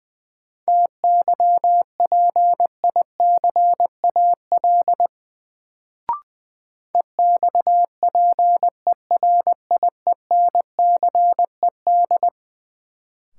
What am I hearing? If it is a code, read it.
Morse code, heard as a keyed tone, about 20 wpm, 700 Hz. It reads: TYPICAL EXPERIENCED
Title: Sentences from Top 1100 Words - New Word Drill Alt - 20wpm